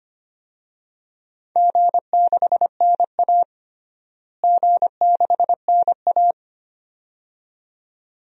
Morse code, heard as a keyed tone, about 25 wpm, 700 Hz. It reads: G6NA G6NA
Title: DXCC-100 Call Signs - 4 Characters - Repeated 2x - 25wpm